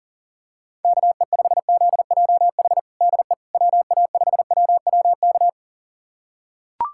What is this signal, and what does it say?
Morse code, heard as a keyed tone, about 40 wpm, 700 Hz. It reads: KE5ZJH DE WA5WWK
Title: USA Callsign DE Callsign - 6 Characters - 40wpm